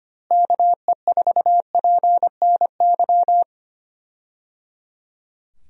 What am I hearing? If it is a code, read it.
Morse code, heard as a keyed tone, about 25 wpm, 700 Hz. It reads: KE4PNY
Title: USA Call Signs - 6 Characters - Repeated 3x - 25wpm